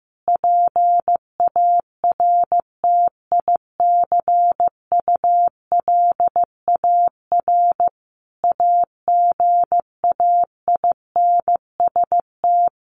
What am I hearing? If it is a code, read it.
Morse code, heard as a keyed tone, about 15 wpm, 700 Hz. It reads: PARTICULAR AGAINST
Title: Sentences from Top 1000 Words 15wpm